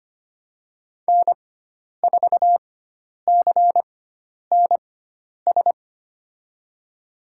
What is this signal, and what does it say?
Morse code, heard as a keyed tone, about 25 wpm, spaced as for 10 wpm, 700 Hz. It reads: N4CNS
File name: Lesson 16g - Letter c - Call Signs 25wpm